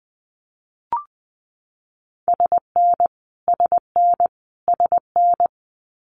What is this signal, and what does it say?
Morse code, heard as a keyed tone, about 20 wpm, 700 Hz. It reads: SN SN SN
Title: Multi-letter Phonemes Repeated 3x 20wpm